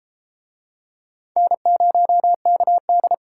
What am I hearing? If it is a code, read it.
Morse code, heard as a keyed tone, about 31 wpm, 700 Hz. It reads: N0KD